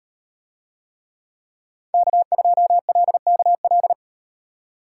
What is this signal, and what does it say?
Morse code, heard as a keyed tone, about 38 wpm, 700 Hz. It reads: K2LKL